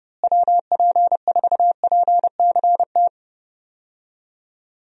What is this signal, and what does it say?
Morse code, heard as a keyed tone, about 32 wpm, 700 Hz. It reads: WP4PCT
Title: USA Call Signs - 6 Characters - Repeated 3x - 30wpm